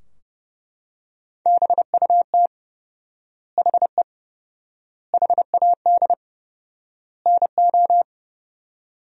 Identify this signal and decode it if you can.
Morse code, heard as a keyed tone, about 32 wpm, 700 Hz. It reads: BUT HE HAD NO
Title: 4x Word-Spacing - Sentences from Top 100 Words - 30wpm